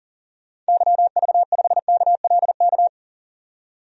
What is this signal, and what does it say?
Morse code, heard as a keyed tone, about 40 wpm, 700 Hz. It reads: YV5KLK
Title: DXCC-100 Call Signs - 6 Characters - 40wpm